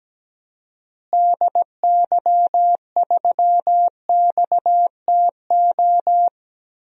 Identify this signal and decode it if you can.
Morse code, heard as a keyed tone, about 17 wpm, 700 Hz. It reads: DY3XTO
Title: DXCC-100 Call Signs - 6 Characters - Repeated 3x - 17wpm